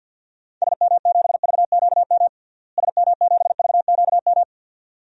Morse code, heard as a keyed tone, about 46 wpm, 700 Hz. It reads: SM7VQM SM7VQM
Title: DXCC-100 Call Signs - 6 Characters - Repeated 2x - 50wpm